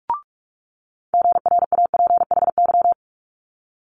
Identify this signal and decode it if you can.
Morse code, heard as a keyed tone, about 46 wpm, 700 Hz. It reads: GRAPHY
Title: Multi-letter Phonemes Quick Fire 45wpm